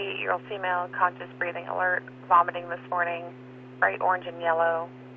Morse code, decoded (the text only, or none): none